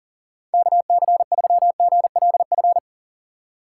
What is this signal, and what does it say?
Morse code, heard as a keyed tone, about 48 wpm, 700 Hz. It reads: KC3GLF
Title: Speed Racing - USA Call Signs 40wpm